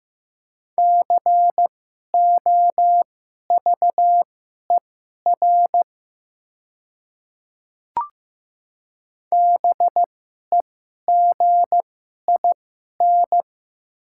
Morse code, heard as a keyed tone, about 15 wpm, spaced as for 11 wpm, 700 Hz. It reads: COVER BEGIN
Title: 2x Character Spacing - Top 300 Words - 15wpm